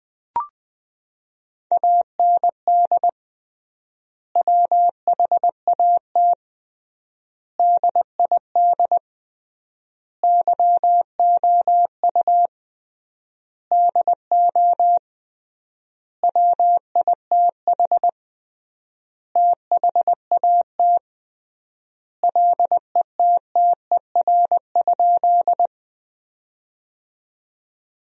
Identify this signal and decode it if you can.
Morse code, heard as a keyed tone, about 21 wpm, 700 Hz. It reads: AND WHAT DID YOU DO WITH THAT LETTER?
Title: 3x Word-Spacing - Sentences from Top 300 Words - 20wpm